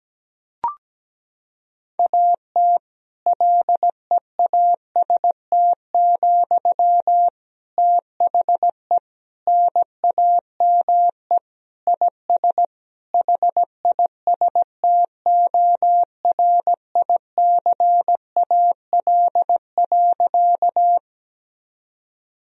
Morse code, heard as a keyed tone, about 17 wpm, 700 Hz. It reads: AT LEAST, THE NAME IS HISTORICAL.
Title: Sentences from Top 1000 Words - Review Part 4 - 17wpm